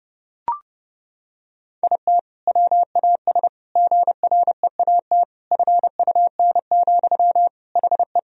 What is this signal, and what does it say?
Morse code, heard as a keyed tone, about 32 wpm, 700 Hz. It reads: IT WAS GREAT FUN, HE
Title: Sentences from Top 1000 Words - Review Part 4 - 30wpm